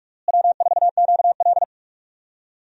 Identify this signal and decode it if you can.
Morse code, heard as a keyed tone, about 46 wpm, 700 Hz. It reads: W4QL
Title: USA Call Signs - 4 Characters - Repeated 3x - 45wpm